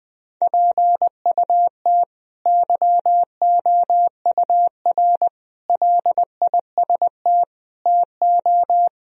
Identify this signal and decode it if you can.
Morse code, heard as a keyed tone, about 20 wpm, 700 Hz. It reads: PUT YOUR LIST TO
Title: Sentences from Top 1000 Words - Review Part 2 - 20wpm